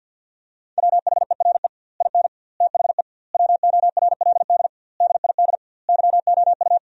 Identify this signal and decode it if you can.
Morse code, heard as a keyed tone, about 46 wpm, 700 Hz. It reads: WHERE IN THE WORLD DID YOU
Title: 2x Word-Spacing - Sentences from Top 200 Words - 50wpm